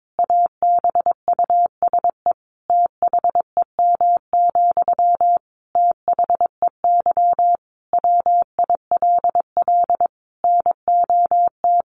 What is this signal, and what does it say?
Morse code, heard as a keyed tone, about 22 wpm, 700 Hz. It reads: ABUSE THEM, THEY WILL NOT
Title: Sentences from Top 1300 Words 22wpm